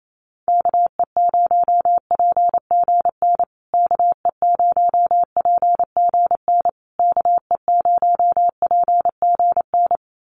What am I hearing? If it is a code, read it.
Morse code, heard as a keyed tone, about 28 wpm, 700 Hz. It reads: KE0PGN KE0PGN KE0PGN
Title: USA Call Signs Repeated 3x 28wpm